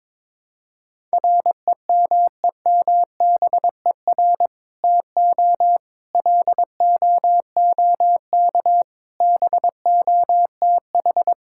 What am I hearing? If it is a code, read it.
Morse code, heard as a keyed tone, about 22 wpm, 700 Hz. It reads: REMEMBER TO LOOK BOTH
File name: practice / 22wpm / Sentences with 2-Word Phrases - 22wpm